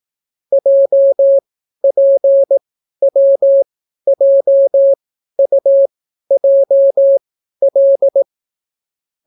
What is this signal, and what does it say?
Morse code, heard as a keyed tone, about 18 wpm, spaced as for 12 wpm, 550 Hz. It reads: JPWJUJL